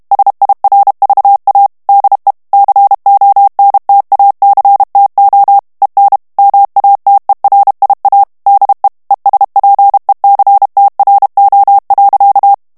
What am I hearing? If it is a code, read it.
Morse code, heard as a keyed tone, about 32 wpm, 800 Hz. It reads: SIRVA DE CONTACTO EN MATERIA DE ESPECTRO.
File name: morse mp3 / ESPAGNOL TEXTE EN CLAIR 1 VITESSE 30 MOTS MINUTES